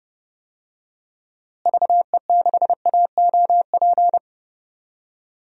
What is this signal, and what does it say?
Morse code, heard as a keyed tone, about 30 wpm, 700 Hz. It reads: VE6AOP
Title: DXCC-100 Call Signs - 6 Characters - Repeated 2x - 30wpm